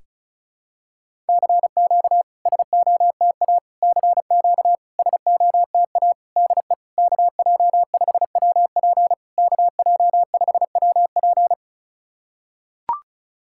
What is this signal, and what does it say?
Morse code, heard as a keyed tone, about 34 wpm, 700 Hz. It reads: CQ SOTA CQ SOTA DE KJ5WP KJ5WP
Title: SOTA QSO Practice 35wpm